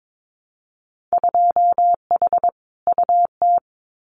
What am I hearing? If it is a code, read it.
Morse code, heard as a keyed tone, about 22 wpm, 700 Hz. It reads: 2H UT
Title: Winter Field Day Exchanges Encore 22wpm